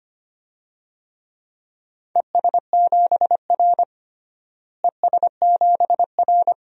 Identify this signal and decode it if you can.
Morse code, heard as a keyed tone, about 25 wpm, 700 Hz. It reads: ES7R ES7R
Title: DXCC-100 Call Signs - 4 Characters - Repeated 2x - 25wpm